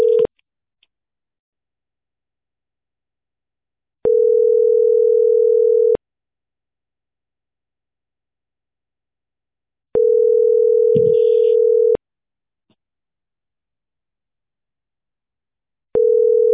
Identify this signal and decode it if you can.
Ringback tone (North America), 440 + 480 Hz.